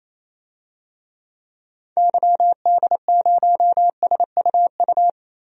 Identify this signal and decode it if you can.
Morse code, heard as a keyed tone, about 28 wpm, 700 Hz. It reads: YD0SUU
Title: DXCC-100 Call Signs - 6 Characters - Repeated 2x - 28wpm